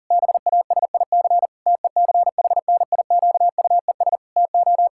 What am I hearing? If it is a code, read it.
Morse code, heard as a keyed tone, about 37 wpm, 700 Hz. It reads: BASIC TECHNIQUES TO
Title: Pangram Challenge v2 - 40wpm